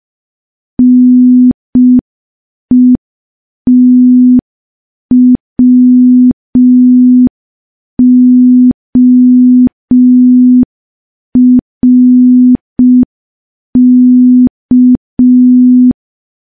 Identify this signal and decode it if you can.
Morse code, heard as a keyed tone, about 5 wpm, 250 Hz. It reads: NETWORK
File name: morse / morse-training-05WPM